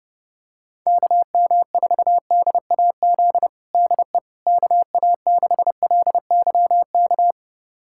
Morse code, heard as a keyed tone, about 30 wpm, 700 Hz. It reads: KM4DAZ DE KA6LYK
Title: USA Callsign DE Callsign 30wpm